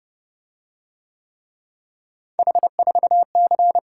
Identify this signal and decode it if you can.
Morse code, heard as a keyed tone, about 28 wpm, 700 Hz. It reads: H4C